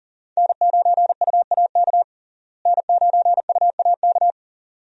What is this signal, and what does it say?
Morse code, heard as a keyed tone, about 37 wpm, 700 Hz. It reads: N9UAK N9UAK
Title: USA Call Signs - 5 Characters - Repeated 2x - 40wpm